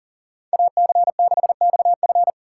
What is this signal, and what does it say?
Morse code, heard as a keyed tone, about 40 wpm, 700 Hz. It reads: AC6XF
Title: USA Call Signs - 5 Characters - 2x Character Spacing - 40wpm